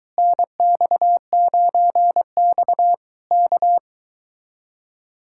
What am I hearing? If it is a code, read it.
Morse code, heard as a keyed tone, about 23 wpm, 700 Hz. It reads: NX9X K